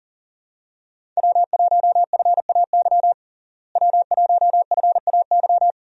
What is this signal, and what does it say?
Morse code, heard as a keyed tone, about 40 wpm, 700 Hz. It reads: W1FAY W1FAY
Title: USA Call Signs - 5 Characters - Repeated 2x - 40wpm